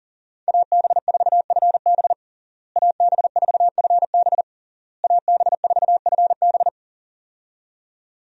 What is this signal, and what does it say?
Morse code, heard as a keyed tone, about 40 wpm, 700 Hz. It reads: AB4FB AB4FB AB4FB
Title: USA Call Signs - 5 Characters - Repeated 3x - 40wpm